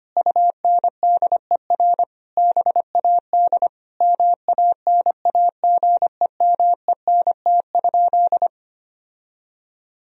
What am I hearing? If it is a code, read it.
Morse code, heard as a keyed tone, about 25 wpm, 700 Hz. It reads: UNDER BAD MANAGEMENT?
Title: Sentences from Top 1000 Words - Review Part 4 - 25wpm